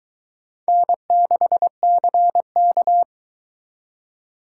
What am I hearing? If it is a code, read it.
Morse code, heard as a keyed tone, about 22 wpm, 700 Hz. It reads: N6CK